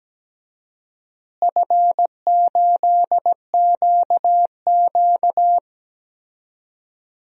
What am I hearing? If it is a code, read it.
Morse code, heard as a keyed tone, about 17 wpm, 700 Hz. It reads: F8QQ
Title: DXCC-100 Call Signs - 4 Characters - Repeated 2x - 17wpm